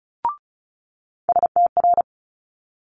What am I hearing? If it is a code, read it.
Morse code, heard as a keyed tone, about 36 wpm, 700 Hz. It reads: STR
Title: Multi-letter Phonemes Quick Fire 35wpm